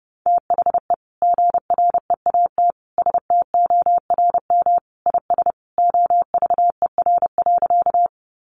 Morse code, heard as a keyed tone, about 30 wpm, 700 Hz. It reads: THE GREAT STORM IS OVER.
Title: Sentences from Top 1500 Words 30wpm